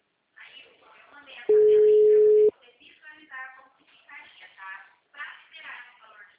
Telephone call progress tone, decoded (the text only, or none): ringback tone